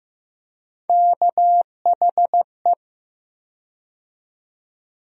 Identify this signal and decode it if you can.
Morse code, heard as a keyed tone, about 16 wpm, 700 Hz. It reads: KHE
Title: USA Call Sign Postfixes - 3 Letters 15wpm